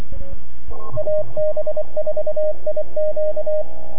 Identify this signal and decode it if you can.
Morse code, heard as a keyed tone, about 24 wpm, 610 Hz, with radio noise. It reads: AB4IQ
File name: NAQPCW-8-2009 / W1UJ NAQP CW 09 0100Z